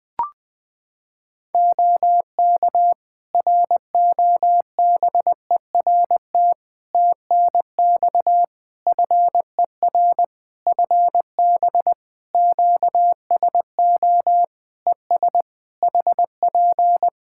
Morse code, heard as a keyed tone, about 20 wpm, 700 Hz. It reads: OK ROBERT TNX FER FB QSO ES HP
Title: Practice QSO - Exchange 4 - Ending - 20wpm